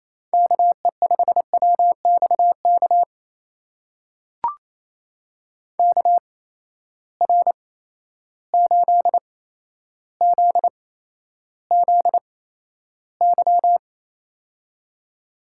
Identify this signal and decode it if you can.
Morse code, heard as a keyed tone, about 28 wpm, spaced as for 12 wpm, 700 Hz. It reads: KE5WXK KR8ZZY
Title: USA Call Signs - 6 Characters - 8x Character Spacing - 28wpm